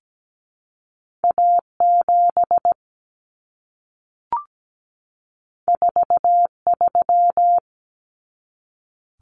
Morse code, heard as a keyed tone, about 17 wpm, 700 Hz. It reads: A7 43
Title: Common Cut Numbers - 2-Digit 17wpm